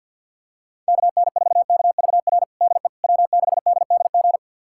Morse code, heard as a keyed tone, about 50 wpm, 700 Hz. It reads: KN4KVR DE W6DDG
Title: USA Callsign DE Callsign 50wpm